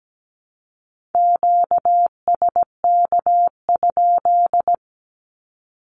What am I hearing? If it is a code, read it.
Morse code, heard as a keyed tone, about 17 wpm, 700 Hz. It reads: QSK?
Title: Q-Codes 17wpm